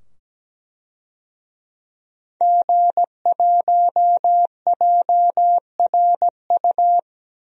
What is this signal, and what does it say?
Morse code, heard as a keyed tone, about 17 wpm, 700 Hz. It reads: G1JRU
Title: DXCC-100 Call Signs - 5 Characters - 17wpm